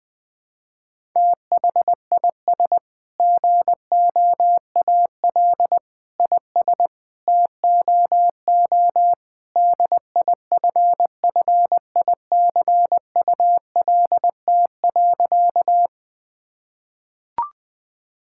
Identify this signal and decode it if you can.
Morse code, heard as a keyed tone, about 21 wpm, 700 Hz. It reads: THIS GOAL IS TOO DIFFICULT.
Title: Sentences from Top 1000 Words - Review Part 5 - 20wpm